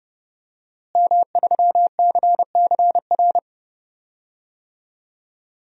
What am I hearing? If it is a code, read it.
Morse code, heard as a keyed tone, about 30 wpm, 700 Hz. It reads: M3CCR
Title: DXCC-100 Call Signs - 5 Characters - Repeated 2x - 30wpm